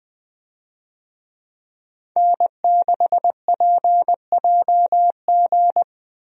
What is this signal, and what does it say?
Morse code, heard as a keyed tone, about 20 wpm, 700 Hz. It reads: N6PJG
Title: USA Call Signs - 5 Characters - 20wpm